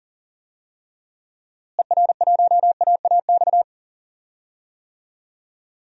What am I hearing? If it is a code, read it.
Morse code, heard as a keyed tone, about 40 wpm, 700 Hz. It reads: ER1AAX
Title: DXCC-100 Call Signs - 6 Characters - Repeated 2x - 40wpm